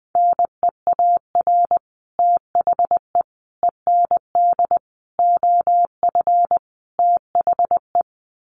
Morse code, heard as a keyed tone, about 20 wpm, 700 Hz. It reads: NEAR THE END OF THE
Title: Sets of 5 Words Encore 20wpm